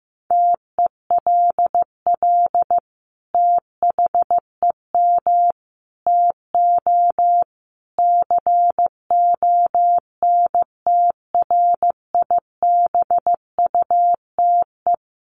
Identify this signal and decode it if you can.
Morse code, heard as a keyed tone, about 14 wpm, 700 Hz. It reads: TELL THEM TO CONTRIBUTE
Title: Sentences from Top 1400 Words 15wpm